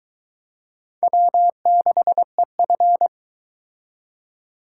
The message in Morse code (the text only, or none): W6EF